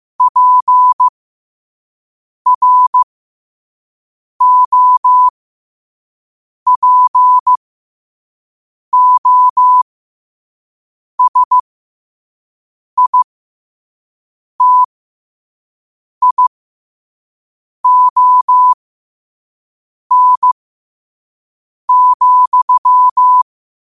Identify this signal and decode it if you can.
Morse code, heard as a keyed tone, about 15 wpm, spaced as for 5 wpm, 1000 Hz. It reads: PROPOSITION,
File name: qotdfiles / 5wpm